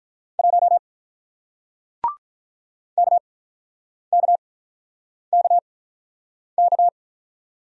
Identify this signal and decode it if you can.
Morse code, heard as a keyed tone, about 46 wpm, spaced as for 9 wpm, 700 Hz. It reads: 1 KKKK